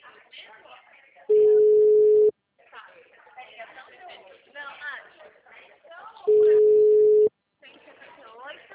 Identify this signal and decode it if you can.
Ringback tone (Europe, 425 Hz), with speech.
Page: Itatiba